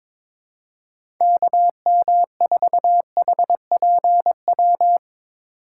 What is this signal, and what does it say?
Morse code, heard as a keyed tone, about 22 wpm, 700 Hz. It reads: KM4HPW
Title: USA Call Signs - 6 Characters - 2x Character Spacing - 22wpm